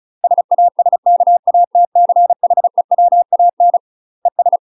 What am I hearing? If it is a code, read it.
Morse code, heard as a keyed tone, about 36 wpm, 700 Hz. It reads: SASKATCHEWAN ES
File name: UK MASTER QSOs TRAINING / 35WPM / UK MASTER QSOs 0007